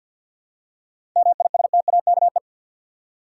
Morse code, heard as a keyed tone, about 46 wpm, 700 Hz. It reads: MISTAKE
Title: Seven Letter Words - 50wpm